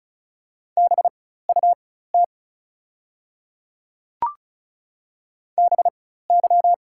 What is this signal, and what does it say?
Morse code, heard as a keyed tone, about 34 wpm, spaced as for 16 wpm, 690 Hz. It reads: BUT BY
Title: 4x Character Spacing - Top 200 Words - 35wpm